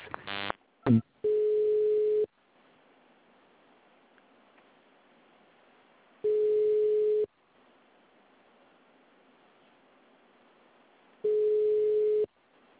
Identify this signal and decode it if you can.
Ringback tone (Europe, 425 Hz).
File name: out- -2003- - - .32